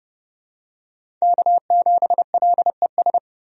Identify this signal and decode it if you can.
Morse code, heard as a keyed tone, about 30 wpm, 700 Hz. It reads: K7LES
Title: USA Call Signs - 5 Characters - 30wpm